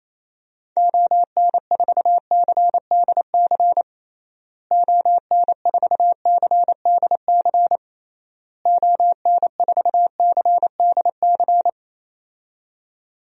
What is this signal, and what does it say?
Morse code, heard as a keyed tone, about 28 wpm, 700 Hz. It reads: ON4CDC ON4CDC ON4CDC
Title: DXCC-100 Call Signs - 6 Characters - Repeated 3x - 28wpm